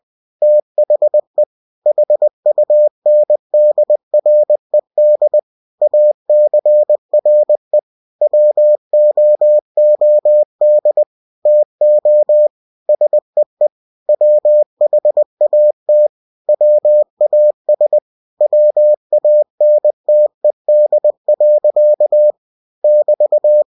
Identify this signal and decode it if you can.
Morse code, heard as a keyed tone, about 20 wpm, 600 Hz. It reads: THE HUNDRED ACRE WOOD TO SEE WHAT WAS WANTED. <BT>
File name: cw-books / winnie-the-pooh / winnie-20wpm-page-0102